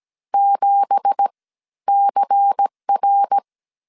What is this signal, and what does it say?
Morse code, heard as a keyed tone, about 17 wpm, 800 Hz. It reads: 7 CR
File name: CW / DET / 1 9 1500C S C 16wpm 2909